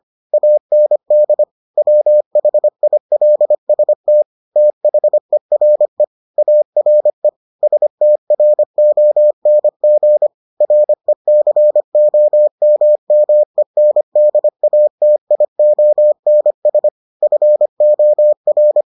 Morse code, heard as a keyed tone, about 25 wpm, 600 Hz. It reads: AND WHILST THERE ARE STRONG RECOMMENDATIONS FOR